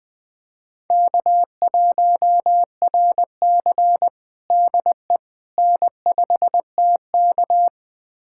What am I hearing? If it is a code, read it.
Morse code, heard as a keyed tone, about 21 wpm, 700 Hz. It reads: K1RC DE N5TK
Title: USA Callsign DE Callsign - 4 Characters - 20wpm